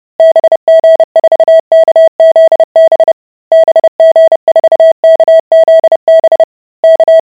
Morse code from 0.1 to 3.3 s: BG4KZB